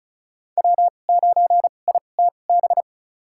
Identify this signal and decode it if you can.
Morse code, heard as a keyed tone, about 36 wpm, spaced as for 25 wpm, 700 Hz. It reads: W9ITB
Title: USA Call Signs - 2x Character Spacing - 35wpm